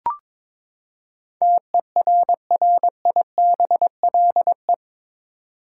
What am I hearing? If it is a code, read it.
Morse code, heard as a keyed tone, about 22 wpm, 700 Hz. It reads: TERRIBLE
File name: practice / 22wpm / Eight Letter Words - 22wpm